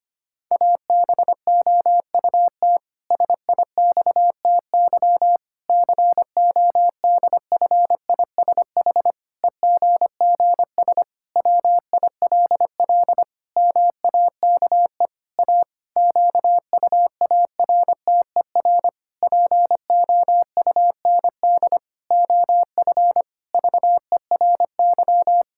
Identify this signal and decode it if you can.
Morse code, heard as a keyed tone, about 25 wpm, 700 Hz. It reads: ABOUT SIXTY CODFISH EGGS WILL MAKE A QUARTER POUND OF VERY
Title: Pangram Challenge v2 - 25wpm